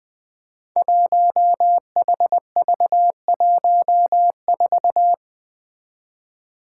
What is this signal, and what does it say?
Morse code, heard as a keyed tone, about 20 wpm, 700 Hz. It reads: 1HV14
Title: Five Challenging Letter-Number Combinations 20wpm